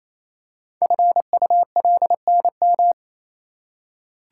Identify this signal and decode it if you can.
Morse code, heard as a keyed tone, about 27 wpm, 700 Hz. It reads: FULNM
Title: Code Groups - 5 Letters - 28wpm